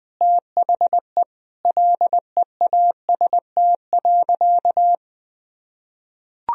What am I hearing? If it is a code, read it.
Morse code, heard as a keyed tone, about 21 wpm, 700 Hz. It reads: THE LEAST.
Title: Sentences from Top 1000 Words - Review Part 4 - 20wpm